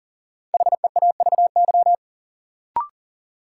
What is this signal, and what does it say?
Morse code, heard as a keyed tone, about 40 wpm, 700 Hz. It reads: HEAVY
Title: Sentences from Top 1100 Words - New Word Drill Alt - 40wpm